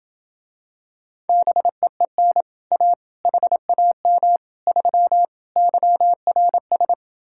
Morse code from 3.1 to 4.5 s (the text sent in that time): HAM